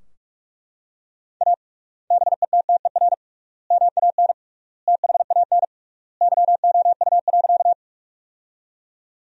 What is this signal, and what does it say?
Morse code, heard as a keyed tone, about 42 wpm, 700 Hz. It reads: A BETTER MAN THAN YOU.
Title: 3x Word-Spacing - Sentences from Top 400 Words - 45wpm